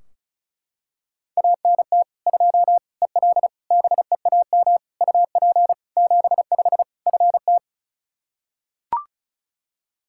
Morse code, heard as a keyed tone, about 36 wpm, 700 Hz. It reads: ANT 2 EL BEAM UP 75 FT
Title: K1USN SST QSO Practice 35wpm